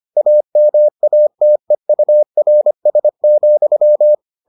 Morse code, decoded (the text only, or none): AMATEURS,